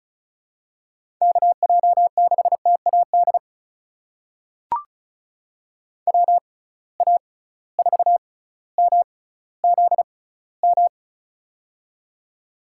Morse code, heard as a keyed tone, about 36 wpm, spaced as for 18 wpm, 700 Hz. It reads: KJ6TAD WA4MZM
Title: USA Call Signs - 6 Characters - 6x Character Spacing - 35wpm